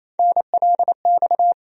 Morse code, heard as a keyed tone, about 28 wpm, 700 Hz. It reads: NLX
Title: USA Call Sign Postfixes - 3 Letters 28wpm